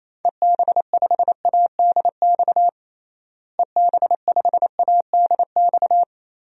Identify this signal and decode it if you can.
Morse code, heard as a keyed tone, about 28 wpm, 700 Hz. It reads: EB5ADX EB5ADX
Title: DXCC-100 Call Signs - 6 Characters - Repeated 2x - 28wpm